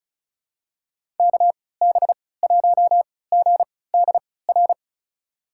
Morse code, heard as a keyed tone, about 36 wpm, spaced as for 20 wpm, 700 Hz. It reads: KB1GDR
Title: USA Call Signs - 6 Characters - 3x Character Spacing - 35wpm